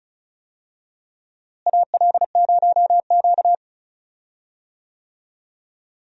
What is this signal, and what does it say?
Morse code, heard as a keyed tone, about 36 wpm, 700 Hz. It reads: AL0Q
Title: USA Call Signs - 4 Characters - Repeated 3x - 35wpm